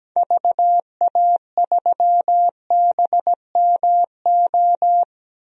Morse code, heard as a keyed tone, about 17 wpm, 700 Hz. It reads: VA3BMO
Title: DXCC-100 Call Signs - 6 Characters - 17wpm